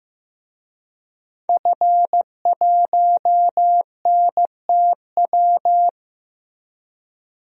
Morse code, heard as a keyed tone, about 15 wpm, 700 Hz. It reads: F1NTW
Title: DXCC-100 Call Signs - 5 Characters - Repeated 2x - 15wpm